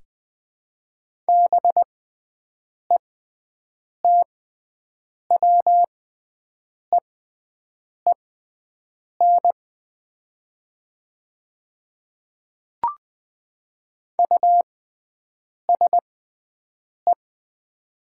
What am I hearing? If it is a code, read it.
Morse code, heard as a keyed tone, about 20 wpm, spaced as for 7 wpm, 700 Hz. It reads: BETWEEN USE
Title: 6x Character Spacing - Top 300 Words - 20wpm